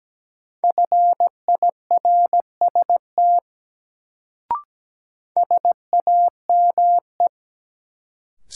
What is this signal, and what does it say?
Morse code, heard as a keyed tone, about 17 wpm, 700 Hz. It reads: FIRST SAME
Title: Sets of 2 Words - New Word Drill - 17wpm